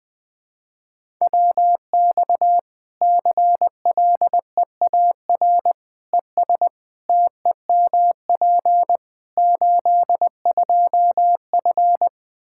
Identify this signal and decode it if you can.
Morse code, heard as a keyed tone, about 21 wpm, 700 Hz. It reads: WX CLEAR ES TEMP 82F
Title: Practice QSO - Exchange 2 - Info - 20wpm